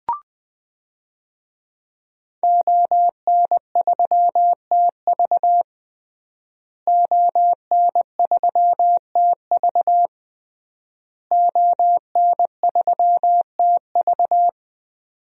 Morse code, heard as a keyed tone, about 20 wpm, 700 Hz. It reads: ON3TV ON3TV ON3TV
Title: DXCC-100 Call Signs - 5 Characters - Repeated 3x - 20wpm